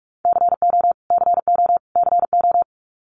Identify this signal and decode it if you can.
Morse code, heard as a keyed tone, about 44 wpm, 700 Hz. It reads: CO CO CO
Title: US States Repeated 3x 45wpm